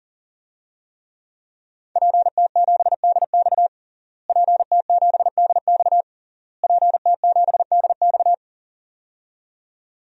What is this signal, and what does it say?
Morse code, heard as a keyed tone, about 40 wpm, 700 Hz. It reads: PT7DX PT7DX PT7DX
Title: DXCC-100 Call Signs - 5 Characters - Repeated 3x - 40wpm